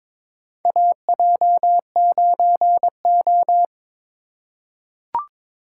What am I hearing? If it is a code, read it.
Morse code, heard as a keyed tone, about 21 wpm, 700 Hz. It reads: AJ9O
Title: USA Call Signs - 4 Characters - 3x Character Spacing - 22wpm